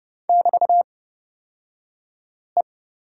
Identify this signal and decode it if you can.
Morse code, heard as a keyed tone, about 29 wpm, spaced as for 5 wpm, 700 Hz. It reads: <BT>E
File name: Lesson 36d - BT - Two Characters 30wpm